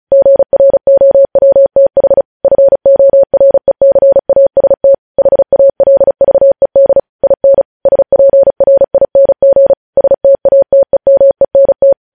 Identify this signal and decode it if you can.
Morse code, heard as a keyed tone, about 36 wpm, 550 Hz. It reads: GROWTH FORECAST HALVED IN SPRING STATEMENT
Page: News Headlines in Morse Code at 35 WPM